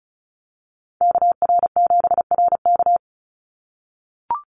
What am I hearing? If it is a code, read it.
Morse code, heard as a keyed tone, about 34 wpm, 700 Hz. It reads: KR7RK
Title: NA SOTA Activator Callsigns 35wpm